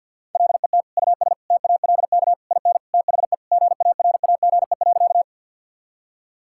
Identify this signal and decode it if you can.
Morse code, heard as a keyed tone, about 50 wpm, 700 Hz. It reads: LET US TALK IN THE GARAGE.
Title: Sentences from Top 1600 Words 50wpm